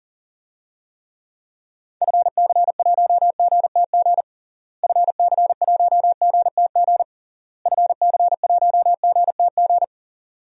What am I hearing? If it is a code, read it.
Morse code, heard as a keyed tone, about 40 wpm, 700 Hz. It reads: FC1GTG FC1GTG FC1GTG
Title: DXCC-100 Call Signs - 6 Characters - Repeated 3x - 40wpm